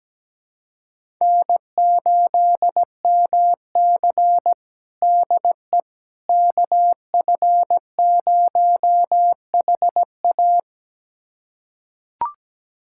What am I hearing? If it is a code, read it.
Morse code, heard as a keyed tone, about 17 wpm, 700 Hz. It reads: N8MC DE KF0HA
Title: USA Callsign DE Callsign 17wpm